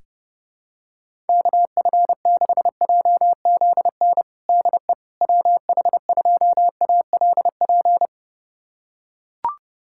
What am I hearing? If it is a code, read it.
Morse code, heard as a keyed tone, about 28 wpm, 700 Hz. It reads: KF6JZN DE WH2ALP
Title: USA Callsign DE Callsign - 6 Characters - 30wpm